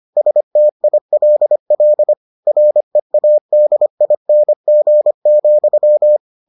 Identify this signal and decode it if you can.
Morse code, heard as a keyed tone, about 25 wpm, 600 Hz. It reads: STILL READING,